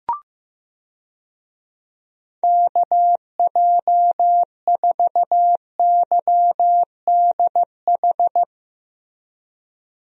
Morse code, heard as a keyed tone, about 15 wpm, 700 Hz. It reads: KJ4YDH
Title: USA Call Signs - 6 Characters - Repeated 2x - 15wpm